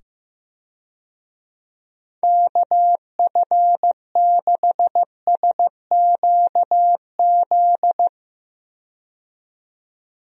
Morse code, heard as a keyed tone, about 15 wpm, 700 Hz. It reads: KF6SQZ
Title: USA Call Signs - 6 Characters - 15wpm